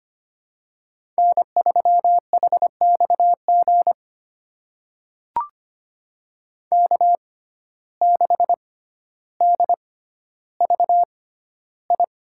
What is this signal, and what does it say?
Morse code, heard as a keyed tone, about 25 wpm, spaced as for 13 wpm, 700 Hz. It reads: N3HXG K6DVI
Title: USA Call Signs - 6x Character Spacing - 25wpm